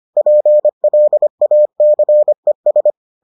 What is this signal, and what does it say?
Morse code, heard as a keyed tone, about 25 wpm, 600 Hz. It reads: PLACES